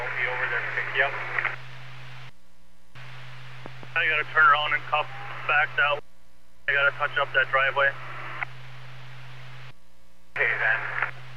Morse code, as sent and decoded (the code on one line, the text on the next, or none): none